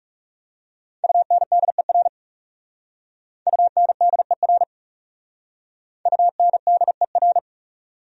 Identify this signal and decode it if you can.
Morse code, heard as a keyed tone, about 40 wpm, 700 Hz. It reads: UNDER UNDER UNDER